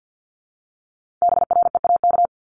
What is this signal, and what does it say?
Morse code, heard as a keyed tone, about 50 wpm, 700 Hz. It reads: BREAK
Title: Multi-letter Phonemes Repeated 3x 50wpm